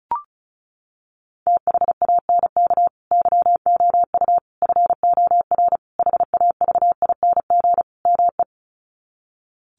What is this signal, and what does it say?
Morse code, heard as a keyed tone, about 34 wpm, 700 Hz. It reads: THANK YOU FOR HAVING ME
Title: Sets of 5 Words Encore 35wpm